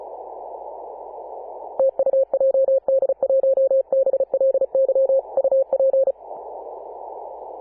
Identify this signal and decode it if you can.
Morse code, heard as a keyed tone, about 34 wpm, 540 Hz. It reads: TU JD1BLY UP
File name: Apr/29 2042z JD1BLY 14.011MHz CW